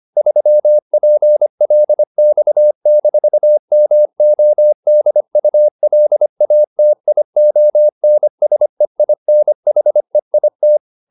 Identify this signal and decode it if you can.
Morse code, heard as a keyed tone, about 25 wpm, 600 Hz. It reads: 3PLX-MODULATIONSEINHEIT